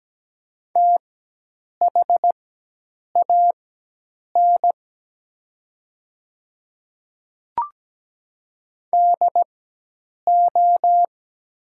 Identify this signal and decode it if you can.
Morse code, heard as a keyed tone, about 17 wpm, spaced as for 8 wpm, 700 Hz. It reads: THAN DO
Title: 4x Character Spacing - Top 100 Words - 17wpm